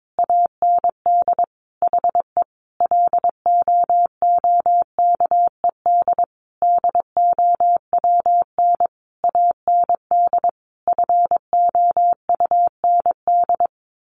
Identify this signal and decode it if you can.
Morse code, heard as a keyed tone, about 22 wpm, 700 Hz. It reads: AND HE LOOKED DOWN AND FOUND
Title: A Sound Of Thunder 22wpm